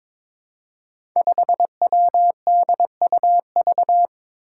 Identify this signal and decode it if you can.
Morse code, heard as a keyed tone, about 22 wpm, 700 Hz. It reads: 5WDUV
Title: Five Challenging Letter-Number Combinations 22wpm